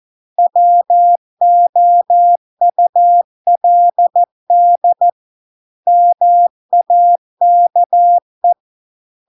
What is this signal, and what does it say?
Morse code, heard as a keyed tone, about 14 wpm, 700 Hz. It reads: WOULD MAKE